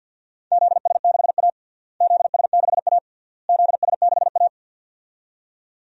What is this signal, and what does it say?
Morse code, heard as a keyed tone, about 47 wpm, 700 Hz. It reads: ZS6A ZS6A ZS6A
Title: DXCC-100 Call Signs - 4 Characters - Repeated 3x - 50wpm